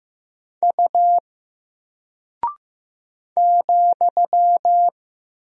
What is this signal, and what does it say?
Morse code, heard as a keyed tone, about 15 wpm, 700 Hz. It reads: U ,
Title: Single Character (Letter:Number:Punctuation:Prosign) 15wpm